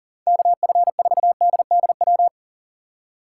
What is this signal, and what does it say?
Morse code, heard as a keyed tone, about 40 wpm, 700 Hz. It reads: KF4DDW
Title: USA Call Signs - 6 Characters - 3x Character Spacing - 40wpm